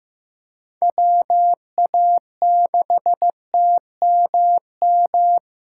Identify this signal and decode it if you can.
Morse code, heard as a keyed tone, about 15 wpm, 700 Hz. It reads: WA6TMM
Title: USA Callsign DE Callsign - 6 Characters - 15wpm